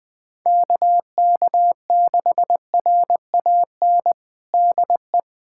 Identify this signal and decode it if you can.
Morse code, heard as a keyed tone, about 20 wpm, 700 Hz. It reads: KK6RAN DE
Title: USA Callsign DE Callsign - 6 Characters - 20wpm